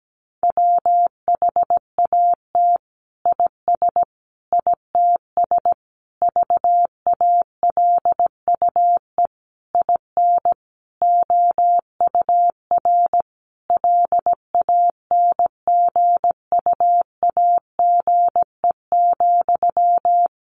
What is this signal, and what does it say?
Morse code, heard as a keyed tone, about 17 wpm, 700 Hz. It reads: WHAT IS ITS VALUE IN OUR LANGUAGE,
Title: Sentences from Top 600 Words 17wpm